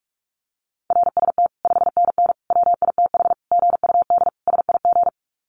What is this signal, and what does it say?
Morse code, heard as a keyed tone, about 44 wpm, 700 Hz. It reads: RST 5NN WITH GUD SIG
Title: Common QSO Phrases 45wpm